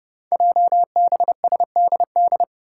Morse code, heard as a keyed tone, about 31 wpm, 700 Hz. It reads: JBSDD
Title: Five Challenging Letter-Number Combinations 30wpm